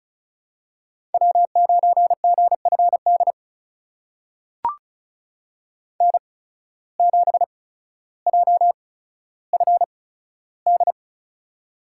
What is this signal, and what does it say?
Morse code, heard as a keyed tone, about 36 wpm, spaced as for 15 wpm, 700 Hz. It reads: W9GFD N7JFD
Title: USA Call Signs - 8x Character Spacing - 35wpm